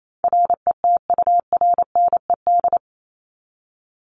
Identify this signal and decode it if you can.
Morse code, heard as a keyed tone, about 28 wpm, 700 Hz. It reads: RETURNED
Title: ED Suffix - Top 100 Words with Base Words 28wpm